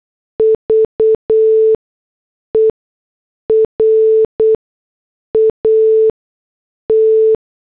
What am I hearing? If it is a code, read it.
Morse code, heard as a keyed tone, about 8 wpm, spaced as for 6 wpm, 440 Hz. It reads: VERAT
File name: morse / headlines